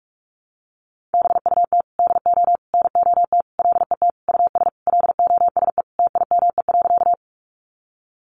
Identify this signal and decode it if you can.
Morse code, heard as a keyed tone, about 44 wpm, 700 Hz. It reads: BUT DO NOT LET US LOSE TIME.
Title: Sentences from Top 1100 Words 45wpm